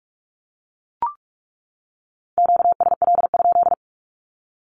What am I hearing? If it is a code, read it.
Morse code, heard as a keyed tone, about 44 wpm, 700 Hz. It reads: QSL?
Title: Q-Codes 45wpm